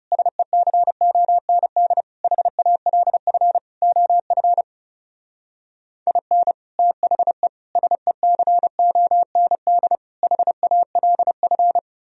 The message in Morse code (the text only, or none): SECOND HALF OF IN THE SECOND HALF